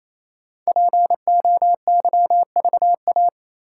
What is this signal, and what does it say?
Morse code, heard as a keyed tone, about 28 wpm, 700 Hz. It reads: POYVA
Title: Code Groups - 5 Letters - 28wpm